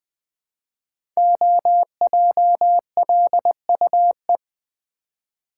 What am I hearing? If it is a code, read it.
Morse code, heard as a keyed tone, about 20 wpm, 700 Hz. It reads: OJLUE
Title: Code Groups - 5 Letter:Numbers - 20wpm